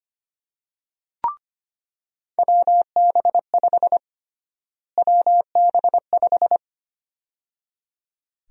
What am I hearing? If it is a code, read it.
Morse code, heard as a keyed tone, about 25 wpm, 700 Hz. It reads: WB5 WB5
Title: USA Call Sign Prefixes - Repeated 2x - 25wpm